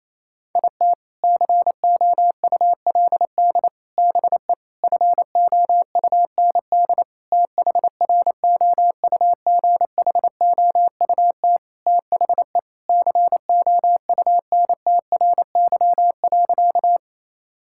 Morse code, heard as a keyed tone, about 27 wpm, 700 Hz. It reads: IT COULD BE FOUND THROUGHOUT THE COUNTRY.
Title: Sentences from Top 1000 Words - Review Part 5 - 28wpm